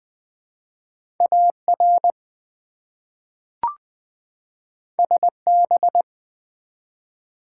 Morse code, heard as a keyed tone, about 20 wpm, 700 Hz. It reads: AR SB
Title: ARRL Sections v2 - 20wpm